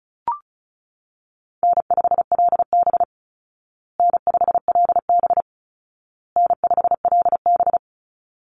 Morse code, heard as a keyed tone, about 34 wpm, 700 Hz. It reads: N5LB N5LB N5LB
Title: Easy Call Signs Repeated 3x - 35wpm